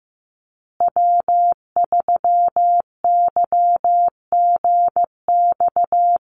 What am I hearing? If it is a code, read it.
Morse code, heard as a keyed tone, about 15 wpm, 700 Hz. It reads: W3YGX
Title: USA Call Signs Repeated 3x 15wpm